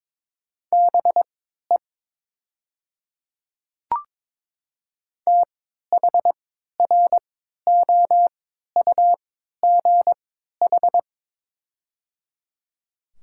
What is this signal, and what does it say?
Morse code, heard as a keyed tone, about 22 wpm, spaced as for 13 wpm, 700 Hz. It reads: BE THROUGH
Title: 3x Character Spacing - Top 200 Words - 22wpm